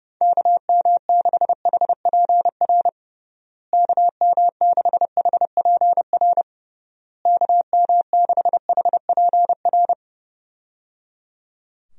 Morse code, heard as a keyed tone, about 30 wpm, 700 Hz. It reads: KM6HPR KM6HPR KM6HPR
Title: USA Call Signs - 6 Characters - Repeated 3x - 30wpm